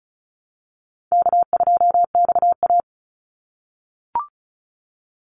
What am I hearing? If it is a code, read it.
Morse code, heard as a keyed tone, about 34 wpm, 700 Hz. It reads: K2XA
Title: Call Signs - 35wpm